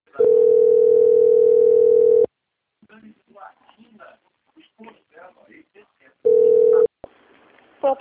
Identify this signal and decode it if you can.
Ringback tone (North America), 440 + 480 Hz, with speech.
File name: out- - - - - .14621